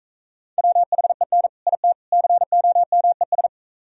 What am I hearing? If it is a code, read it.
Morse code, heard as a keyed tone, about 44 wpm, 700 Hz. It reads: WHEN IT COMES